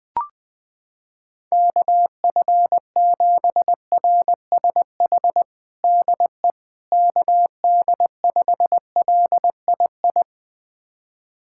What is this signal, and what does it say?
Morse code, heard as a keyed tone, about 20 wpm, 700 Hz. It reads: KF7RSH DE KD5LII
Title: USA Callsign DE Callsign 20wpm